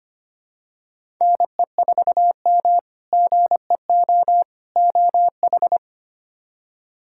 Morse code, heard as a keyed tone, about 25 wpm, 700 Hz. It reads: NE4M GEO OH
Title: K1USN SST QSO Practice - 25wpm